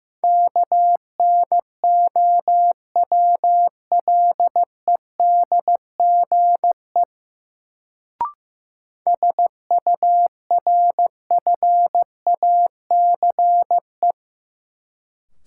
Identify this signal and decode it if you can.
Morse code, heard as a keyed tone, about 15 wpm, 700 Hz. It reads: KNOWLEDGE SURFACE
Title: Sentences from Top 600 Words - New Word Drill - 15wpm